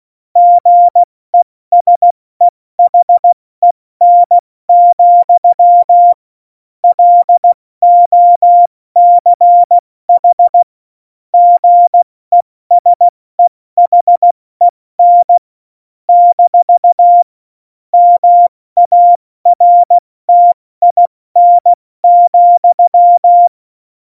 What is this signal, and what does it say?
Morse code, heard as a keyed tone, about 16 wpm, spaced as for 14 wpm, 700 Hz. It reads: GESEHEN, LOCH GESEHEN - MARTIN,
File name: Morsekurs / Witze / Witz 05. Dezember 2017, 70 F80